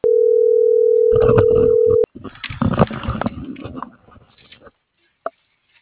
Ringback tone (North America), 440 + 480 Hz.